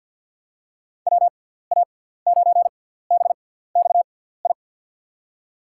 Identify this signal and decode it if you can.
Morse code, heard as a keyed tone, about 48 wpm, spaced as for 17 wpm, 700 Hz. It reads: WA9BXI
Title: USA Call Signs - 6 Characters - 6x Character Spacing - 50wpm